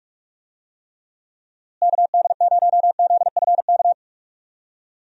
Morse code, heard as a keyed tone, about 43 wpm, 700 Hz. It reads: KD0ZFK
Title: USA Call Signs - 6 Characters - Repeated 3x - 45wpm